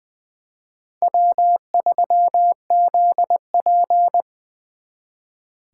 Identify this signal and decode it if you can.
Morse code, heard as a keyed tone, about 24 wpm, 700 Hz. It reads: W3ZP
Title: Speed Racing - USA 1x2 Call Signs 20wpm